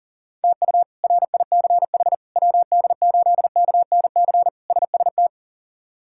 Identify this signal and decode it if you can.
Morse code, heard as a keyed tone, about 40 wpm, 700 Hz. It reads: TU RICH WD8KNC SST
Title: K1USN SST QSO Practice - 40wpm